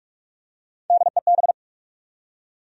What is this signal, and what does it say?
Morse code, heard as a keyed tone, about 45 wpm, 700 Hz. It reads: DEB
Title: USA Common Names - Max 3 Letters - 45wpm